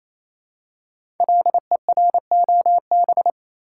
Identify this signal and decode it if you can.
Morse code, heard as a keyed tone, about 28 wpm, 700 Hz. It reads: LEROB
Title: Code Groups - 5 Letters - 28wpm